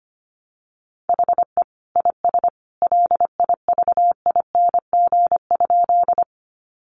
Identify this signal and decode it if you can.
Morse code, heard as a keyed tone, about 25 wpm, 700 Hz. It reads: HE IS LIVING?
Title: Sentences from Top 1100 Words 25wpm